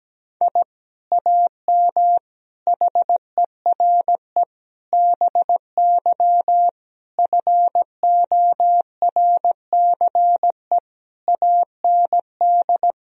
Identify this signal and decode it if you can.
Morse code, heard as a keyed tone, about 17 wpm, 700 Hz. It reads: I AM HERE BY FORCE AND
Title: Sentences from Top 1000 Words - Review Part 3 - 17wpm